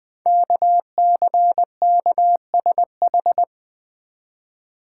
Morse code, heard as a keyed tone, about 20 wpm, 700 Hz. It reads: KCKSH
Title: Code Groups - 5 Letters - 20wpm